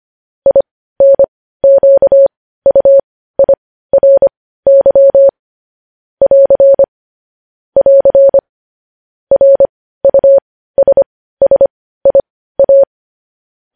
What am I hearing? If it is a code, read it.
Morse code, heard as a keyed tone, about 25 wpm, spaced as for 15 wpm, 550 Hz. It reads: INQUIRY <AR> <AR> RUSSIA
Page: news headlines in morse code at 15 wpm